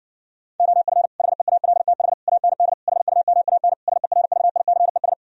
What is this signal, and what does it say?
Morse code, heard as a keyed tone, about 69 wpm, 700 Hz. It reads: OF HEALTH AND HUMAN SERVICES